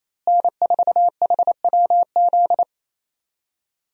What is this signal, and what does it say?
Morse code, heard as a keyed tone, about 28 wpm, 700 Hz. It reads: N4HWZ
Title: USA Call Signs - 5 Characters - Repeated 2x - 28wpm